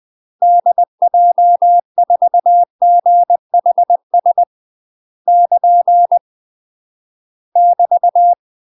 Morse code, heard as a keyed tone, about 20 wpm, 700 Hz. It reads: DJ4GHS <KN> <BT>
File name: UK MASTER QSOs TRAINING / 20WPM / UK MASTER QSOs 0018